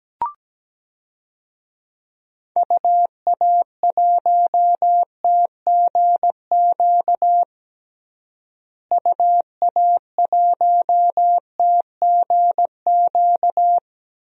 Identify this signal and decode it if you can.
Morse code, heard as a keyed tone, about 17 wpm, 700 Hz. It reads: UA1TGQ UA1TGQ
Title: DXCC-100 Call Signs - 6 Characters - Repeated 2x - 17wpm